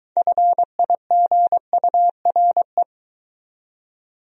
Morse code, heard as a keyed tone, about 24 wpm, 700 Hz. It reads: FIGURE